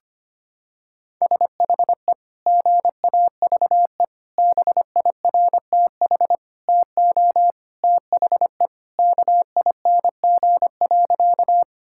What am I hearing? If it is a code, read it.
Morse code, heard as a keyed tone, about 25 wpm, 700 Hz. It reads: SHE GAVE BIRTH TO THE KING.
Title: Sentences from Top 1000 Words - Review Part 5 - 25wpm